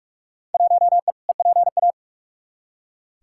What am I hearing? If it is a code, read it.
Morse code, heard as a keyed tone, about 43 wpm, 700 Hz. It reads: 1E EPA
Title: Field Day Exchanges - 45wpm